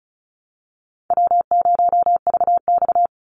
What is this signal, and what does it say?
Morse code, heard as a keyed tone, about 34 wpm, 700 Hz. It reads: W0VX
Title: Call Signs Repeated 3x - 35wpm